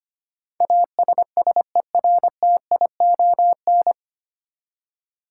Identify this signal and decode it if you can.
Morse code, heard as a keyed tone, about 25 wpm, 700 Hz. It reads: ASSERTION
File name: Nine Letter Words - 25wpm